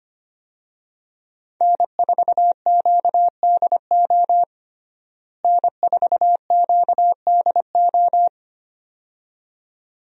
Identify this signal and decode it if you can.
Morse code, heard as a keyed tone, about 25 wpm, 700 Hz. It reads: N4QDO N4QDO
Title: USA Call Signs - 5 Characters - Repeated 2x - 25wpm